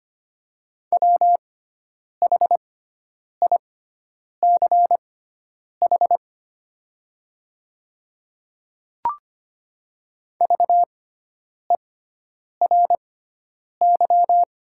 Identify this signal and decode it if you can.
Morse code, heard as a keyed tone, about 25 wpm, spaced as for 9 wpm, 700 Hz. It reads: WHICH VERY
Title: 6x Character Spacing - Top 200 Words - 25wpm